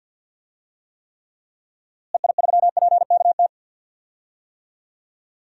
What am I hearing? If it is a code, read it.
Morse code, heard as a keyed tone, about 51 wpm, 700 Hz. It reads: EI3PKT
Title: DXCC-100 Call Signs - 6 Characters - Repeated 2x - 50wpm